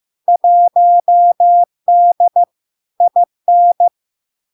Morse code, heard as a keyed tone, about 15 wpm, 700 Hz. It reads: 1D IN
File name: LARK-FD-2022-Log-in-CW-15wpm-700-Hz